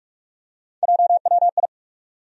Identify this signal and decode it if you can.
Morse code, heard as a keyed tone, about 46 wpm, 700 Hz. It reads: JWI
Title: USA Call Sign Postfixes - 3 Letters 45wpm